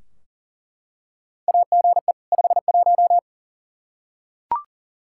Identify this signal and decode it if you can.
Morse code, heard as a keyed tone, about 40 wpm, 700 Hz. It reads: AGE 51
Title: Practice QSO - Exchange 3 - More Info - 40wpm